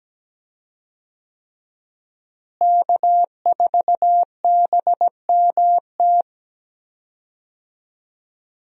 Morse code, heard as a keyed tone, about 17 wpm, 700 Hz. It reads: K4BMT
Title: USA Call Signs - 5 Characters - Repeated 3x - 17wpm